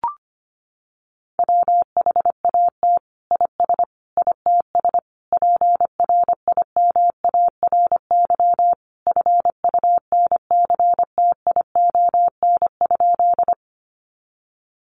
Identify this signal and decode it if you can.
Morse code, heard as a keyed tone, about 25 wpm, 700 Hz. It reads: WHAT IS ITS PRIMARY FUNCTION?
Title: Sentences from Top 1100 Words 25wpm